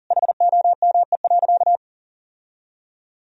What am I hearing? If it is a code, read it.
Morse code, heard as a keyed tone, about 40 wpm, 700 Hz. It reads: HOME.
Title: Sentences from Top 1000 Words - Review Part 3 - 40wpm